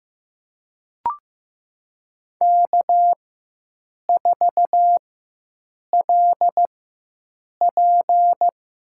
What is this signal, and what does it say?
Morse code, heard as a keyed tone, about 15 wpm, spaced as for 7 wpm, 700 Hz. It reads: K4LP
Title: USA Call Signs - 5 Characters - 4x Character Spacing - 15wpm